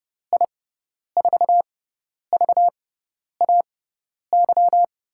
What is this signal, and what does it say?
Morse code, heard as a keyed tone, about 31 wpm, spaced as for 10 wpm, 700 Hz. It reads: I4VAY
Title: USA Call Signs - 6x Character Spacing - 30wpm